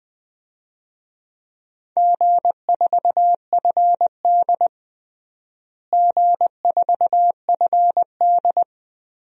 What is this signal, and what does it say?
Morse code, heard as a keyed tone, about 20 wpm, 700 Hz. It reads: G4FD G4FD
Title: DXCC-100 Call Signs - 4 Characters - Repeated 2x - 20wpm